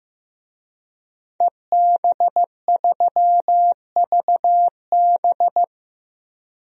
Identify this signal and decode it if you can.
Morse code, heard as a keyed tone, about 15 wpm, 700 Hz. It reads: EB3VB
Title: DXCC-100 Call Signs - 5 Characters - 15wpm